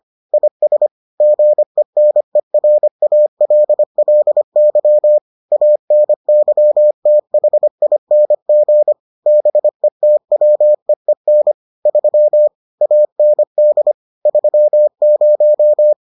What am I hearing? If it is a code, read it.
Morse code, heard as a keyed tone, about 25 wpm, 600 Hz. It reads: IS GENERALLY ANYTHING BETWEEN 3 AND 30